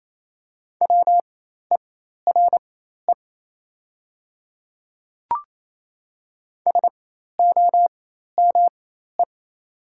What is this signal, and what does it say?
Morse code, heard as a keyed tone, about 28 wpm, spaced as for 13 wpm, 690 Hz. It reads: WERE SOME
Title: 4x Character Spacing - Top 300 Words - 28wpm